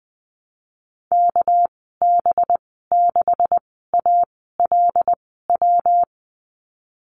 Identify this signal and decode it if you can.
Morse code, heard as a keyed tone, about 20 wpm, spaced as for 15 wpm, 700 Hz. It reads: KB6ALW
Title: 200 Lesson 32g - Letter k - Call Signs 20wpm